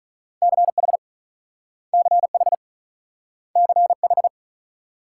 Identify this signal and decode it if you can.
Morse code, heard as a keyed tone, about 43 wpm, 700 Hz. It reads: CH CH CH
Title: Speed-Racing Multi-letter Phonemes - 35wpm